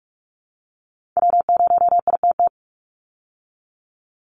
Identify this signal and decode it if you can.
Morse code, heard as a keyed tone, about 44 wpm, 700 Hz. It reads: W0ITT
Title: Call Signs Repeated 3x - 45wpm